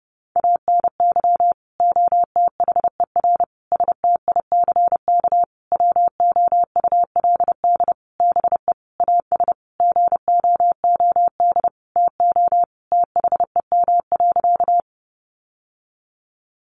Morse code, heard as a keyed tone, about 28 wpm, 700 Hz. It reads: ANY OTHER STICK WOULD BE AS GOOD TO THEM.
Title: Sentences from Top 1200 Words 30wpm